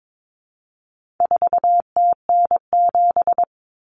Morse code, heard as a keyed tone, about 22 wpm, 700 Hz. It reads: 4TN7
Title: Common Cut Numbers - 4-Digit 22wpm